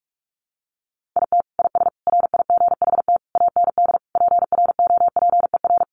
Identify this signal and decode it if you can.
Morse code, heard as a keyed tone, about 44 wpm, 700 Hz. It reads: IT IS RIGHT AND PROPER
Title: Sentences from Top 1500 Words 45wpm